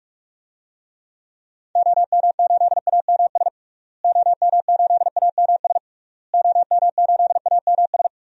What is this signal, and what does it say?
Morse code, heard as a keyed tone, about 46 wpm, 700 Hz. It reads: OM8AMS OM8AMS OM8AMS
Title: DXCC-100 Call Signs - 6 Characters - Repeated 3x - 45wpm